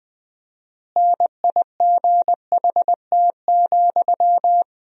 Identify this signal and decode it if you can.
Morse code, heard as a keyed tone, about 20 wpm, 700 Hz. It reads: NIGHT,
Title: 4x Word-Spacing - Sentences from Top 300 Words - 20wpm